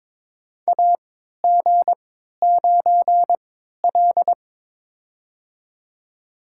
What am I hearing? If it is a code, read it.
Morse code, heard as a keyed tone, about 22 wpm, spaced as for 12 wpm, 700 Hz. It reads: AG9L
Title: USA Call Signs - 4 Characters - 3x Character Spacing - 22wpm